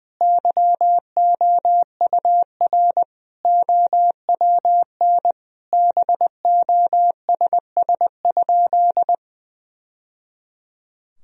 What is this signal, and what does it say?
Morse code, heard as a keyed tone, about 21 wpm, 700 Hz. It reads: YOUR OWN BOSS?
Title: Sentences from Top 1000 Words - Review Part 4 - 20wpm